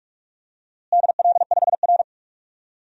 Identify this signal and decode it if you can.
Morse code, heard as a keyed tone, about 42 wpm, 700 Hz. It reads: DL5R
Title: DXCC-100 Call Signs - 4 Characters - Repeated 3x - 45wpm